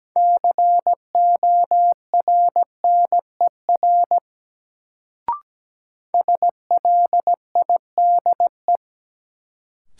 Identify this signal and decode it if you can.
Morse code, heard as a keyed tone, about 17 wpm, 700 Hz. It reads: CORNER SLIDE
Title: Sentences from Top 1600 Words - New Word Drill - 17wpm